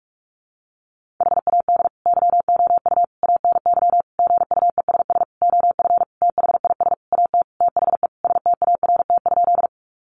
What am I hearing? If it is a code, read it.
Morse code, heard as a keyed tone, about 44 wpm, 700 Hz. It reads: HAD YOU ANY GUESS OF THIS AT THE START?
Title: Sentences from Top 1500 Words 45wpm